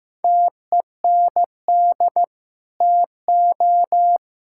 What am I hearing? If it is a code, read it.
Morse code, heard as a keyed tone, about 16 wpm, 700 Hz. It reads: TEND TO
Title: Sentences from Top 1000 Words - Review Part 5 - 15wpm